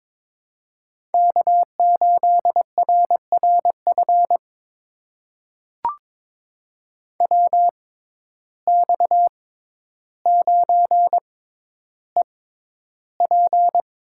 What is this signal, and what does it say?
Morse code, heard as a keyed tone, about 22 wpm, spaced as for 11 wpm, 700 Hz. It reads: K8RRF WX9EP
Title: USA Call Signs - 5 Characters - 6x Character Spacing - 22wpm